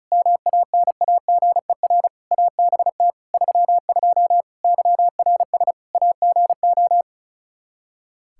Morse code, heard as a keyed tone, about 34 wpm, 700 Hz. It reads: MANAGER ABT 32 YRS AGO
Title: Practice QSO - Exchange 3 - More Info - 35wpm